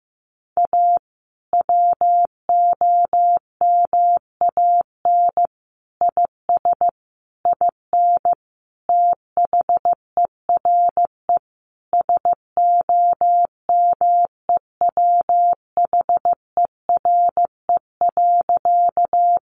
Morse code, heard as a keyed tone, about 15 wpm, 700 Hz. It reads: A WOMAN IS IN THERE SOMEWHERE.
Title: Sentences from Top 1400 Words 15wpm